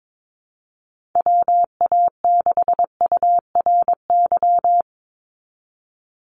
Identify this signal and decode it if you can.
Morse code, heard as a keyed tone, about 22 wpm, 700 Hz. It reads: WA6URY
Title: Call Signs - 22wpm